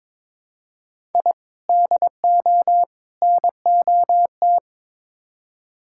Morse code, heard as a keyed tone, about 21 wpm, 700 Hz. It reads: I DO NOT
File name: practice / 22wpm / Build A Phrase - 5-Word Phrases - 22wpm